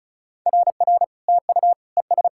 Morse code, heard as a keyed tone, about 36 wpm, 700 Hz. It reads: RR TU ES
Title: SOTA QSO Practice 35wpm